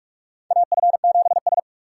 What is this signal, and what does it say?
Morse code, heard as a keyed tone, about 46 wpm, 700 Hz. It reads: AF7S
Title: USA Call Signs - 4 Characters - 2x Character Spacing - 45wpm